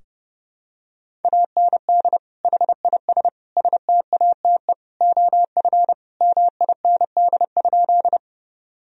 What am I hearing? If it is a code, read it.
Morse code, heard as a keyed tone, about 32 wpm, 700 Hz. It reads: AND HIS STATE OF MIND?
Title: Sentences from Top 1000 Words - Review Part 2 - 30wpm